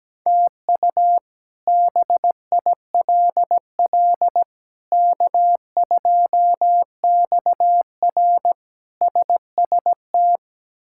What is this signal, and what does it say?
Morse code, heard as a keyed tone, about 17 wpm, 700 Hz. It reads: TU BILL K2XR SST
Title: K1USN SST QSO Practice - 17wpm